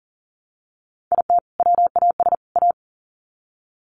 Morse code, heard as a keyed tone, about 40 wpm, 700 Hz. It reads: IT WAS A
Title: Sets of 3 Words 40wpm